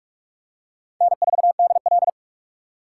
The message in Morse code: N4DL

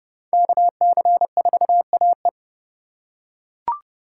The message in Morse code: KC4AE

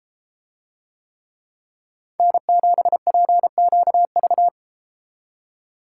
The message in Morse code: N7PQV

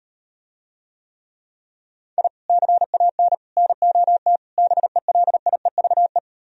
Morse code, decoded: I CAN NOT BELIEVE

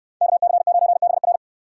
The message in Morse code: KC9BR

